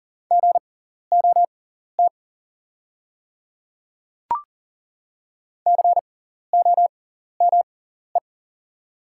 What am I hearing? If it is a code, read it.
Morse code, heard as a keyed tone, about 37 wpm, spaced as for 14 wpm, 690 Hz. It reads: GOT COME
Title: 6x Character Spacing - Top 300 Words - 40wpm